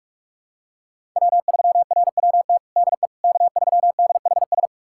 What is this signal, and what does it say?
Morse code, heard as a keyed tone, about 46 wpm, 700 Hz. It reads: W3RWT DE K3DHS
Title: USA Callsign DE Callsign - 5 Characters - 45wpm